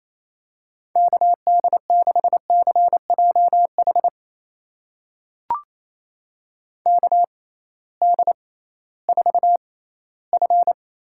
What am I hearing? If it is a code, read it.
Morse code, heard as a keyed tone, about 28 wpm, spaced as for 16 wpm, 700 Hz. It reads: KD6CJH KD4F
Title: USA Call Signs - 6 Characters - 6x Character Spacing - 28wpm